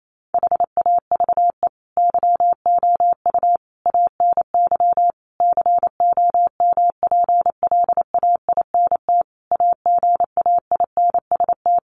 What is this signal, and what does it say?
Morse code, heard as a keyed tone, about 28 wpm, 700 Hz. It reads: HAVE YOU ANY COMPLAINT AGAINST
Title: Sentences from Top 1400 Words 28wpm